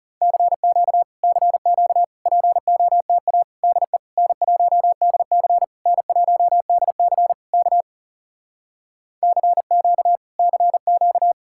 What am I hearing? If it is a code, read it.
Morse code, heard as a keyed tone, about 40 wpm, 700 Hz. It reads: CQ CQ POTA DE N1DC N1DC K CQ CQ